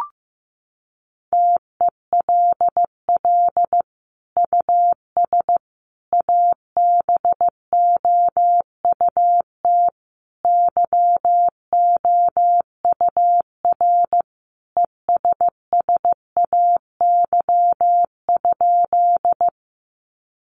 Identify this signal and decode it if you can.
Morse code, heard as a keyed tone, about 15 wpm, 700 Hz. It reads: TELL US ABOUT YOUR ESSAY?
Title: Sentences from Top 1500 Words 15wpm